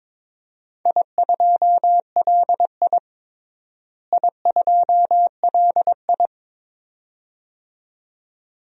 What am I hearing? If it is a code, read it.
Morse code, heard as a keyed tone, about 22 wpm, 700 Hz. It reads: I2LI I2LI
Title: DXCC-100 Call Signs - 4 Characters - Repeated 2x - 22wpm